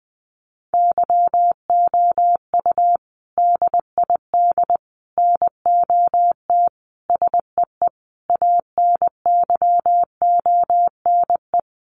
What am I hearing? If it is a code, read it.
Morse code, heard as a keyed tone, about 20 wpm, 700 Hz. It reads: YOU DID NOT SEE ANYONE
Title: Sentences from Top 900 Words 20wpm